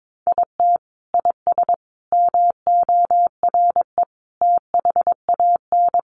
Morse code, heard as a keyed tone, about 22 wpm, 700 Hz. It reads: IT IS MORE THAN
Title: Sentences from Top 1400 Words 22wpm